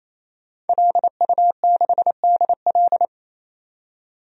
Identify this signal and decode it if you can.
Morse code, heard as a keyed tone, about 27 wpm, 700 Hz. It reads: LU6DL
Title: DXCC-100 Call Signs - 5 Characters - 28wpm